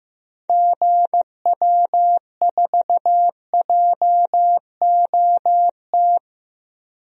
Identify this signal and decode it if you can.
Morse code, heard as a keyed tone, about 15 wpm, 700 Hz. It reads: GW4JOT
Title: DXCC-100 Call Signs - 6 Characters - 15wpm